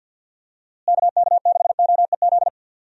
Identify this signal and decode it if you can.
Morse code, heard as a keyed tone, about 47 wpm, 700 Hz. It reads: KK6OEZ
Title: USA Call Signs - 6 Characters - Repeated 3x - 50wpm